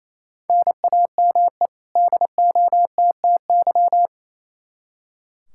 Morse code, heard as a keyed tone, about 28 wpm, 700 Hz. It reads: NAME DOTTY
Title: K1USN SST QSO Practice 28wpm